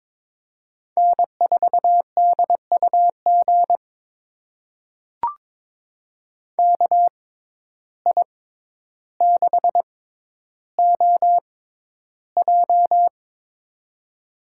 Morse code, heard as a keyed tone, about 22 wpm, spaced as for 11 wpm, 700 Hz. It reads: N4DUG KI6OJ
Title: USA Call Signs - 5 Characters - 6x Character Spacing - 22wpm